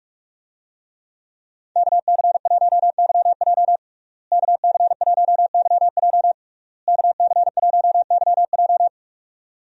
Morse code, heard as a keyed tone, about 44 wpm, 700 Hz. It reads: KC1YJ KC1YJ KC1YJ
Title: USA Call Signs - 5 Characters - Repeated 3x - 45wpm